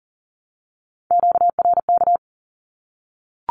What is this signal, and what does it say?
Morse code, heard as a keyed tone, about 40 wpm, 700 Hz. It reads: QRK
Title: Q-Codes 40wpm